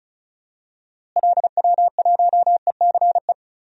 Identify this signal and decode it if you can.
Morse code, heard as a keyed tone, about 36 wpm, 700 Hz. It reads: LW1ECE
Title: DXCC-100 Call Signs - 6 Characters - Repeated 3x - 35wpm